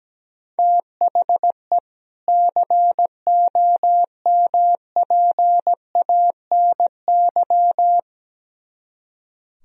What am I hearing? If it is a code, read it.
Morse code, heard as a keyed tone, about 17 wpm, 700 Hz. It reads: THE COMPANY
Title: Speed-Racing - Sets of 2 Words 17wpm